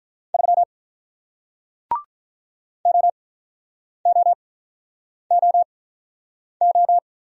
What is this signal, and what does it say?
Morse code, heard as a keyed tone, about 48 wpm, spaced as for 9 wpm, 700 Hz. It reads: 3 OOOO